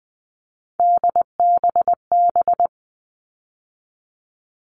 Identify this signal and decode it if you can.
Morse code, heard as a keyed tone, about 20 wpm, 700 Hz. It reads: DBB
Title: All Cut Numbers - 3-Digit 20wpm